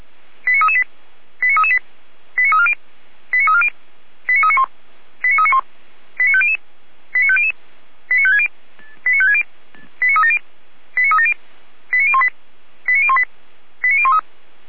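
Five-tone selective call (ZVEI), heard as quick runs of five tones, tones of about 70 ms. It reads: 89338 89338 89340 89340 89391 89391 89500 89500 89560 89560 89370 89370 89018 89018 89013
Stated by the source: ZVEI five-tone call received on an SDR.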